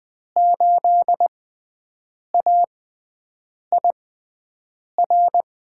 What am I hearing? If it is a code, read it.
Morse code, heard as a keyed tone, about 20 wpm, spaced as for 7 wpm, 700 Hz. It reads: 8AIR
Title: USA Call Signs - 5 Characters - 6x Character Spacing - 20wpm